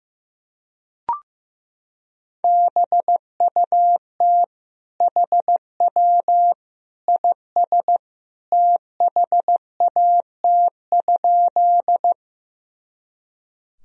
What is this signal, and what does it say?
Morse code, heard as a keyed tone, about 15 wpm, 700 Hz. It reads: BUT HW IS THAT?
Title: Sentences from Top 100 Words and Abbreviations 15wpm